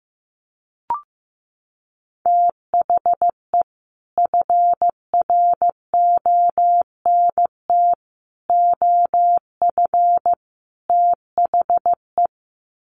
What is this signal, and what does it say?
Morse code, heard as a keyed tone, about 15 wpm, 700 Hz. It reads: THE FRONT OF THE
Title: Sets of 4 Words 15wpm